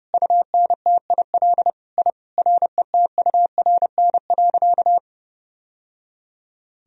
Morse code, heard as a keyed tone, about 30 wpm, 700 Hz. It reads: UNTIL I RETURN.
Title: Sentences from Top 1000 Words - Review Part 1 - 30wpm